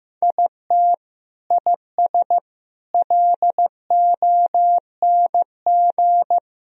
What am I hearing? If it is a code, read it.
Morse code, heard as a keyed tone, about 15 wpm, 700 Hz. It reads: IT IS LONG
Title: Sentences from Top 1000 Words - Review Part 1 - 15wpm